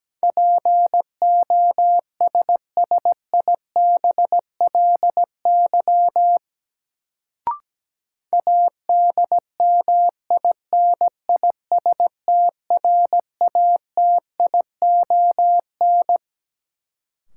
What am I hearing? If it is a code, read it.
Morse code, heard as a keyed tone, about 17 wpm, 700 Hz. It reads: POSSIBLY ADMINISTRATION
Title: Sentences from Top 1300 Words - New Word Drill - 17wpm